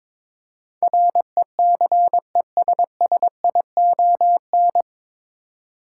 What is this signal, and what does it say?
Morse code, heard as a keyed tone, about 22 wpm, 700 Hz. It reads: RECESSION
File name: 9 Letter Words - First 3 Letters Given - 22wpm